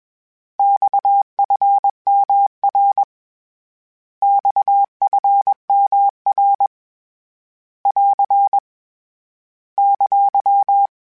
Morse code, heard as a keyed tone, about 21 wpm, 800 Hz. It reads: XFMR XFMR <AR> !
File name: Abreviations / Abreviations mp3s / Abreviations@20